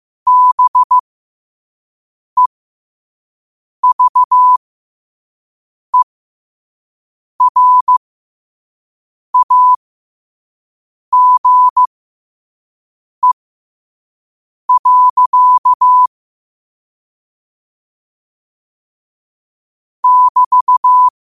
Morse code, heard as a keyed tone, about 15 wpm, spaced as for 5 wpm, 1000 Hz. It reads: BEVERAGE. <BT>
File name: qotdfiles / 5wpm